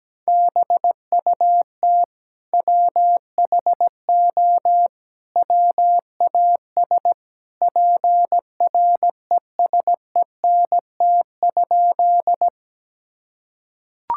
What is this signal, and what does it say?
Morse code, heard as a keyed tone, about 17 wpm, 700 Hz. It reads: BUT WHO WAS PRESENT?
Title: Sentences from Top 1000 Words - Review Part 5 - 17wpm